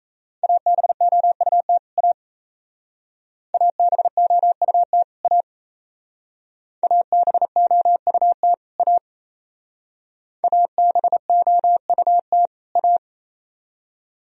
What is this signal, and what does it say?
Morse code, heard as a keyed tone, about 35 wpm, 700 Hz. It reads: ABOUT A ABOUT A ABOUT A ABOUT A